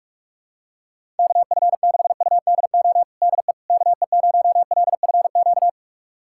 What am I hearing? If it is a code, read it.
Morse code, heard as a keyed tone, about 46 wpm, 700 Hz. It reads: KF6UDO DE KE0LFQ
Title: USA Callsign DE Callsign 45wpm